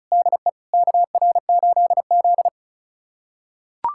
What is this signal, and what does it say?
Morse code, heard as a keyed tone, about 36 wpm, 700 Hz. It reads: DE KR8Z
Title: USA Callsign DE Callsign - 4 Characters - 35wpm